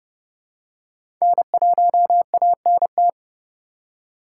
Morse code, heard as a keyed tone, about 30 wpm, 700 Hz. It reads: N1ANT
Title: USA Call Signs - 5 Characters - 2x Character Spacing - 30wpm